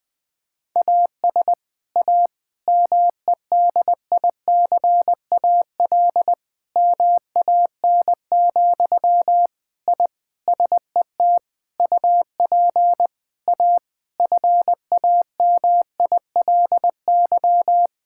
Morse code, heard as a keyed tone, about 20 wpm, 700 Hz. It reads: AS A MEDICAL MAN, I SET UP A FAMILY
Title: Sentences from Top 1000 Words - Review Part 5 - 20wpm